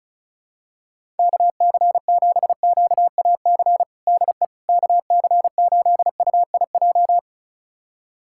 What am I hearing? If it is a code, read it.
Morse code, heard as a keyed tone, about 36 wpm, 700 Hz. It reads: KC7QAC DE KC8UIJ
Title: USA Callsign DE Callsign - 6 Characters - 35wpm